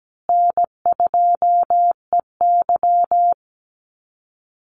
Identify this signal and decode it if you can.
Morse code, heard as a keyed tone, about 17 wpm, 700 Hz. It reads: N2EY
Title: Call Signs Repeated 3x - 17wpm